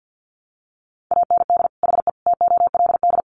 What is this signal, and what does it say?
Morse code, heard as a keyed tone, about 50 wpm, 700 Hz. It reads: AND HE TOLD
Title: Sentences from Top 500 Words 50wpm